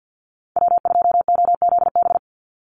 Morse code, heard as a keyed tone, about 48 wpm, 700 Hz. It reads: W2OZB
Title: Call Signs Repeated 3x - 50wpm